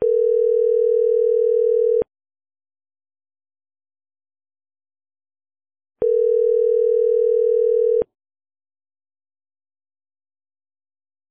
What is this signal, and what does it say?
Ringback tone (North America), 440 + 480 Hz.